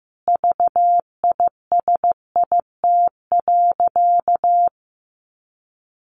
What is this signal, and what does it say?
Morse code, heard as a keyed tone, about 15 wpm, 700 Hz. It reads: VISIT.
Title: Sentences from Top 1200 Words 15wpm